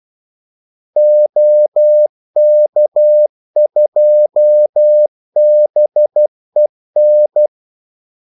Morse code, heard as a keyed tone, about 12 wpm, 600 Hz. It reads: OK2BEN
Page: 2 ekstra ordmellemrum mellem kaldesignalerne